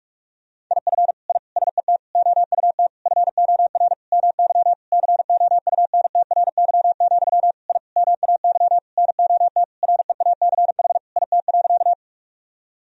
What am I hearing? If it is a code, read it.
Morse code, heard as a keyed tone, about 44 wpm, 700 Hz. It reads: IF I SET OUT FOR MY COUNTRY, I MAY NOT REACH IT.
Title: Sentences from Top 1000 Words - Review Part 2 - 45wpm